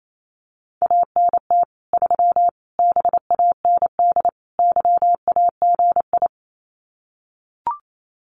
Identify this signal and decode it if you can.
Morse code, heard as a keyed tone, about 28 wpm, 700 Hz. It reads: ANT 3 BAND YAGI
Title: Common QSO Phrases 28wpm